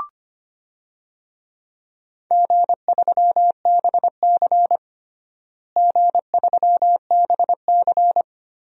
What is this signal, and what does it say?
Morse code, heard as a keyed tone, about 25 wpm, 700 Hz. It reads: G3BC G3BC
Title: DXCC-100 Call Signs - 4 Characters - Repeated 2x - 25wpm